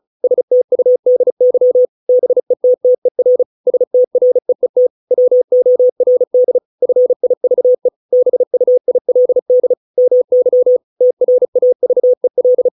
Morse code, heard as a keyed tone, about 34 wpm, 500 Hz. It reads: STUDY BETTER STREET WORD FIVE BUILD MY TRAVEL